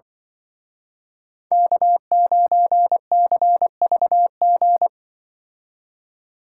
Morse code, heard as a keyed tone, about 24 wpm, 700 Hz. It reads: K9CVG
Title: Speed Racing - USA Call Signs 20wpm